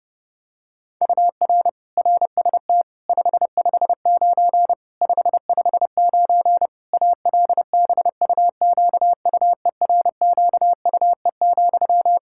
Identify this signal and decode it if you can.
Morse code, heard as a keyed tone, about 30 wpm, 700 Hz. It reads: UR RST 559 559 ALBUQUERQUE,
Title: Practice QSO - Exchange 1 - Beginning - 30wpm